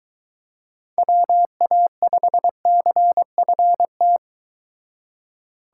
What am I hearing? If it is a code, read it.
Morse code, heard as a keyed tone, about 23 wpm, 700 Hz. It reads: WA5CFT